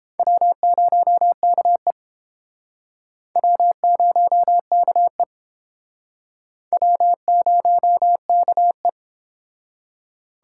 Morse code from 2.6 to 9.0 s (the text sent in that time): W0KE W0KE